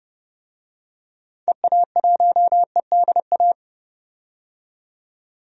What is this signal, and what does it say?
Morse code, heard as a keyed tone, about 32 wpm, 700 Hz. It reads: EA1EDA
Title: DXCC-100 Call Signs - 6 Characters - 30wpm